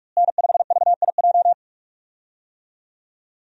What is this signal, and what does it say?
Morse code, heard as a keyed tone, about 43 wpm, 700 Hz. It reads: N5VIJ
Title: USA Call Signs - 5 Characters - Repeated 2x - 45wpm